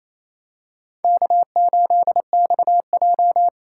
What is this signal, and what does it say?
Morse code, heard as a keyed tone, about 28 wpm, 700 Hz. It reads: K8XJ
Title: K1USN SST QSO Practice - 28wpm